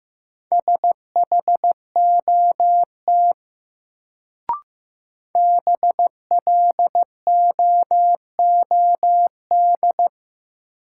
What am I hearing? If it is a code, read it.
Morse code, heard as a keyed tone, about 15 wpm, 700 Hz. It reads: SHOT BLOOD
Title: Sentences from Top 1100 Words - New Word Drill Alt - 15wpm